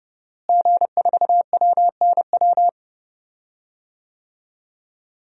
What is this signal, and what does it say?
Morse code, heard as a keyed tone, about 30 wpm, 700 Hz. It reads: G4WNW
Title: DXCC-100 Call Signs - 5 Characters - Repeated 2x - 30wpm